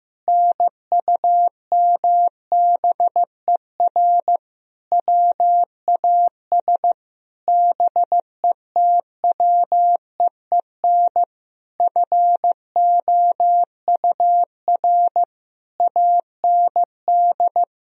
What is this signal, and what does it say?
Morse code, heard as a keyed tone, about 15 wpm, 700 Hz. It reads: NUMBER WAS BETWEEN FOUR AND
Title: Sentences from Top 1000 Words - Review Part 2 - 15wpm